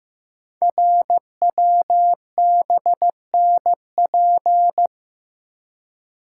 Morse code, heard as a keyed tone, about 15 wpm, 700 Hz. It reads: RWBNP
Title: Code Groups - 5 Letters - 15wpm